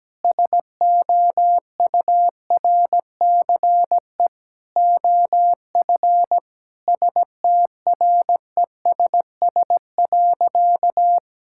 Morse code, heard as a keyed tone, about 17 wpm, 700 Hz. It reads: SOURCE OF STRESS.
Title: Sentences from Top 1000 Words - Review Part 6 - 17wpm